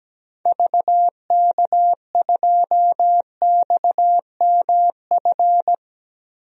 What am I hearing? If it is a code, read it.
Morse code, heard as a keyed tone, about 17 wpm, 700 Hz. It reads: VK2XMF
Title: DXCC-100 Call Signs - 6 Characters - 17wpm